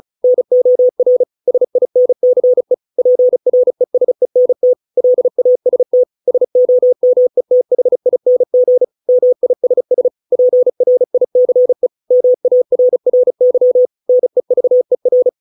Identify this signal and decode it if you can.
Morse code, heard as a keyed tone, about 34 wpm, 500 Hz. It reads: NOR SINCE PRESENT LAST SOMETHING MISS PRICE MARRY NEVER